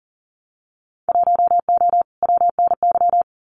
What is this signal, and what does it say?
Morse code, heard as a keyed tone, about 41 wpm, 700 Hz. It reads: 1O WNY
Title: Winter Field Day Exchanges 40wpm